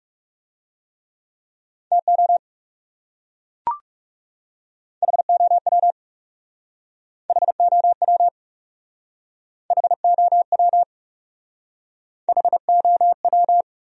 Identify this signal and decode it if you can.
Morse code, heard as a keyed tone, about 38 wpm, 700 Hz. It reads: TO HOW HOW HOW HOW